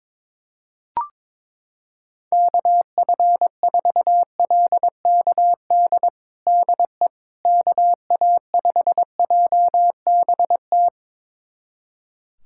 Morse code, heard as a keyed tone, about 22 wpm, 700 Hz. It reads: KF4LKD DE KA5JBT
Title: USA Callsign DE Callsign - 6 Characters - 22wpm